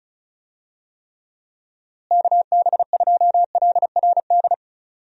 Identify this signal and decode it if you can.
Morse code, heard as a keyed tone, about 36 wpm, 700 Hz. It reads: KB2LRD
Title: USA Call Signs - 6 Characters - Repeated 3x - 35wpm